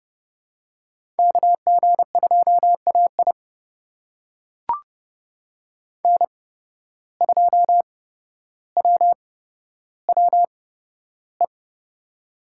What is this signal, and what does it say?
Morse code, heard as a keyed tone, about 31 wpm, spaced as for 13 wpm, 700 Hz. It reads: KG2AI N2WWE
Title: USA Call Signs - 5 Characters - 8x Character Spacing - 30wpm